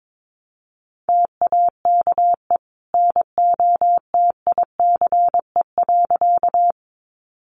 Morse code, heard as a keyed tone, about 22 wpm, 700 Hz. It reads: TAKE NOTICE.
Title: Sentences from Top 400 Words 22wpm